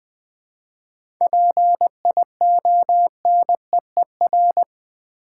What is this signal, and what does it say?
Morse code, heard as a keyed tone, about 20 wpm, 700 Hz. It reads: PIONEER
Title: 7 Letter Words - First 3 Letters Given - 20wpm